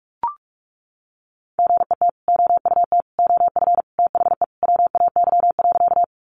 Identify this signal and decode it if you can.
Morse code, heard as a keyed tone, about 44 wpm, 700 Hz. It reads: GET OUT OF THE WAY.
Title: Sentences from Top 200 Words 45wpm